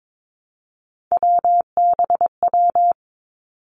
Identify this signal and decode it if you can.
Morse code, heard as a keyed tone, about 22 wpm, 700 Hz. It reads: WBW
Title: Three Challenging Letter-Number Combinations 22wpm